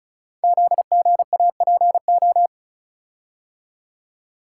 Morse code, heard as a keyed tone, about 36 wpm, 700 Hz. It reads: ZGAPO
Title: Code Groups - 5 Letters - 35wpm